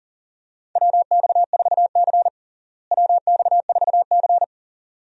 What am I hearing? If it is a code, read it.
Morse code, heard as a keyed tone, about 40 wpm, 700 Hz. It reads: WX4C WX4C
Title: USA Call Signs - 4 Characters - Repeated 2x - 40wpm